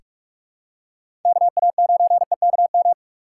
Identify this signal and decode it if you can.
Morse code, heard as a keyed tone, about 46 wpm, 700 Hz. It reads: KA9EKM
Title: USA Call Signs - 3x Character Spacing - 45wpm